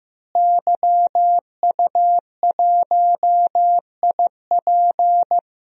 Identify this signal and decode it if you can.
Morse code, heard as a keyed tone, about 16 wpm, 700 Hz. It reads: YU1IP
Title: DXCC-100 Call Signs - 5 Characters - Repeated 3x - 15wpm